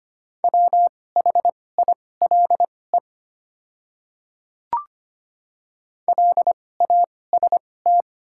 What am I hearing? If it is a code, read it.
Morse code, heard as a keyed tone, about 25 wpm, spaced as for 18 wpm, 700 Hz. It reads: WHILE LAST
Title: 2x Character Spacing - Top 500 Words - 25wpm